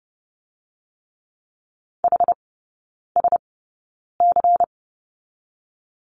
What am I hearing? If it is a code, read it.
Morse code, heard as a keyed tone, about 30 wpm, spaced as for 9 wpm, 700 Hz. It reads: HSC
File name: Lesson 17e - Letter u - Three Characters 30wpm